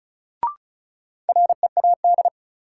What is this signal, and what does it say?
Morse code, heard as a keyed tone, about 36 wpm, 700 Hz. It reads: READ
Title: Sentences from Top 200 Words - New Word Drill - 35wpm